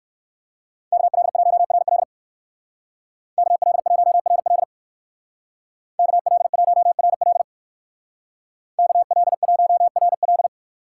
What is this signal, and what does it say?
Morse code, heard as a keyed tone, about 54 wpm, 700 Hz. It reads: KL1RL KL1RL KL1RL KL1RL
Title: Speed Racing - USA Call Signs 45wpm